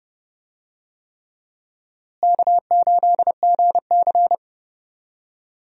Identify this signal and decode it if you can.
Morse code, heard as a keyed tone, about 31 wpm, 700 Hz. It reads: K8GC
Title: USA Call Signs - 4 Characters - Repeated 3x - 30wpm